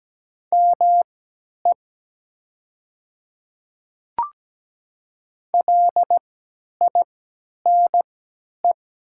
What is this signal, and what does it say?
Morse code, heard as a keyed tone, about 17 wpm, spaced as for 10 wpm, 700 Hz. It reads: ME LINE
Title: 3x Character Spacing - Top 200 Words - 17wpm